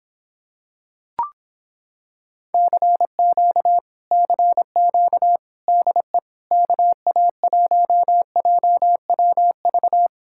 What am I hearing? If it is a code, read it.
Morse code, heard as a keyed tone, about 26 wpm, 700 Hz. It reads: CQ CQ DE KA1JWV